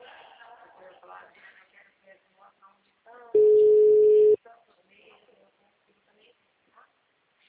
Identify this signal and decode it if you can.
Ringback tone (Europe, 425 Hz), with speech.
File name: out- -2041- - - .25829